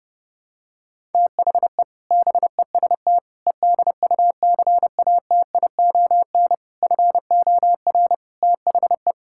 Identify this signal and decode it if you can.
Morse code, heard as a keyed tone, about 30 wpm, 700 Hz. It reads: THE BEST EDUCATION FOR THE
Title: Sentences from Top 1000 Words - Review Part 6 - 30wpm